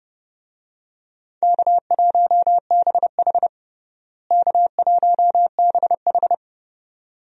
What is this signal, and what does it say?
Morse code, heard as a keyed tone, about 32 wpm, 700 Hz. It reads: K1BH K1BH
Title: USA Call Signs - 4 Characters - Repeated 2x - 30wpm